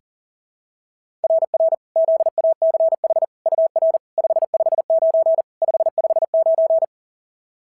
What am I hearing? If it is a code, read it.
Morse code, heard as a keyed tone, about 40 wpm, 690 Hz. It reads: RR ZACH UR 559 559
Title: SOTA QSO Practice 40wpm